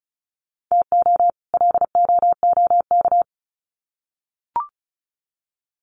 Morse code, heard as a keyed tone, about 34 wpm, 700 Hz. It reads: TO LOOK
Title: Sets of 2 Words 35wpm